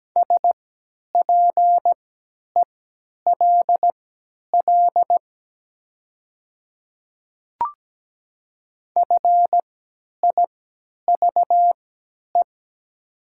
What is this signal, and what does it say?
Morse code, heard as a keyed tone, about 17 wpm, spaced as for 10 wpm, 700 Hz. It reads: SPELL FIVE
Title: 3x Character Spacing - Top 500 Words - 17wpm